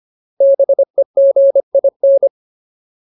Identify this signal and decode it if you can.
Morse code, heard as a keyed tone, about 25 wpm, 550 Hz. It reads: BEGIN
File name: Ultimate Word List / 25WPM Ultimate